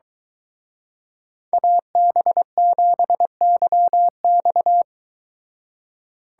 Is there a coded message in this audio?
Yes: Morse code from 1.5 to 4.8 s, heard as a keyed tone, about 22 wpm, 700 Hz. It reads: AB7YX